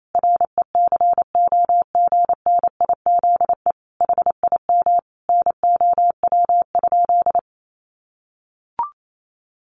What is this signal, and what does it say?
Morse code, heard as a keyed tone, about 28 wpm, 700 Hz. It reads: RECOGNIZE HIM NOW?
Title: Sentences from Top 1400 Words 28wpm